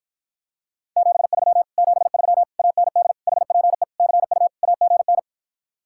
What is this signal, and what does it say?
Morse code, heard as a keyed tone, about 43 wpm, 700 Hz. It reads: 73 73 AND HPE CU AGN
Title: Speed-Racing Common QSO Phrases - 35wpm